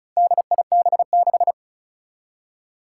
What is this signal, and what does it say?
Morse code, heard as a keyed tone, about 34 wpm, 700 Hz. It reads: DIB6
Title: Four Challenging Letter-Number Combinations 35wpm